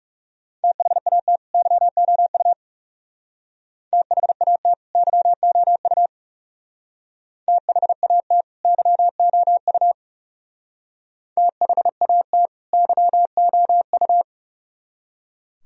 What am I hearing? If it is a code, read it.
Morse code, heard as a keyed tone, about 36 wpm, 700 Hz. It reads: THAT YOU THAT YOU THAT YOU THAT YOU
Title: Speed-Racing - Sets of 2 Words 30wpm